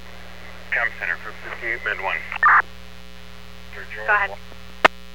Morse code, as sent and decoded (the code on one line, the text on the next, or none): none